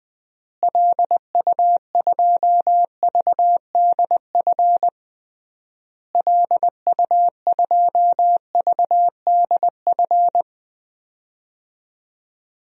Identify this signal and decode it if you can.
Morse code, heard as a keyed tone, about 21 wpm, 700 Hz. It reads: LU2VDF LU2VDF
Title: DXCC-100 Call Signs - 6 Characters - Repeated 3x - 20wpm